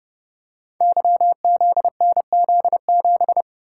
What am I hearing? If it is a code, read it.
Morse code, heard as a keyed tone, about 30 wpm, 700 Hz. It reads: YZNZ7
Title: Code Groups - 5 Letter:Numbers - 30wpm